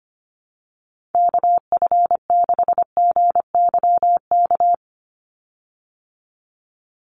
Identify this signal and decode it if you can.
Morse code, heard as a keyed tone, about 25 wpm, 700 Hz. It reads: KF6GYK
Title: USA Call Signs 25wpm